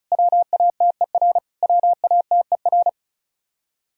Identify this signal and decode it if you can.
Morse code, heard as a keyed tone, about 34 wpm, 700 Hz. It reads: WATER WATER
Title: Top 100 Words Repeated 35wpm